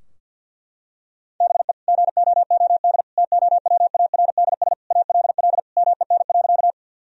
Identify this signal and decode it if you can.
Morse code, heard as a keyed tone, about 50 wpm, 700 Hz. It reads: BE GOOD TOWARDS ALL MEN.
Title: Sentences from Top 1000 Words - Review Part 2 - 50wpm